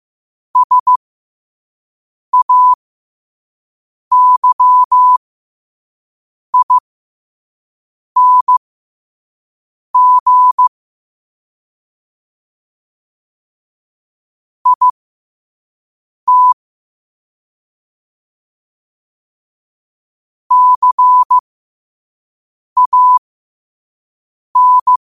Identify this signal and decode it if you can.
Morse code, heard as a keyed tone, about 15 wpm, spaced as for 5 wpm, 1000 Hz. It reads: SAYING IT CAN
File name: qotdfiles / 5wpm